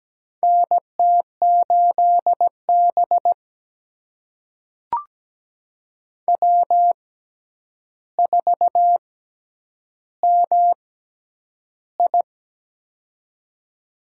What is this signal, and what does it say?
Morse code, heard as a keyed tone, about 17 wpm, spaced as for 9 wpm, 700 Hz. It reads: NT8B W4MI
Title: USA Call Signs - 4 Characters - 6x Character Spacing - 17wpm